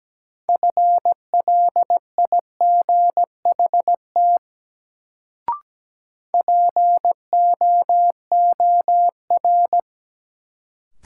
Morse code, heard as a keyed tone, about 17 wpm, 700 Hz. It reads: FLIGHT POOR
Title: Sentences from Top 1000 Words - New Word Drill - 17wpm